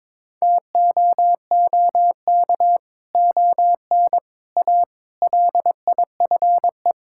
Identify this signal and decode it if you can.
Morse code, heard as a keyed tone, about 22 wpm, 700 Hz. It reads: TOOK ON A LIFE
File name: practice / 22wpm / Sets of 7 Words - 22wpm